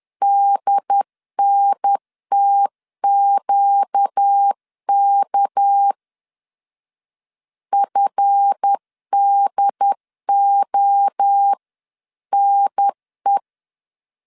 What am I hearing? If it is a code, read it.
Morse code, heard as a keyed tone, about 10 wpm, 800 Hz. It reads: DNTQK FDONE
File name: LEZIONI / 7 GF9Z / 7 GF 10wpm 2514